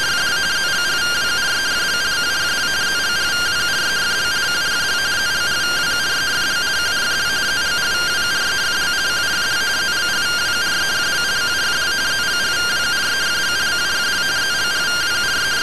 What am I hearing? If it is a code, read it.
Two-tone teleprinter signal, about 170 Hz shift.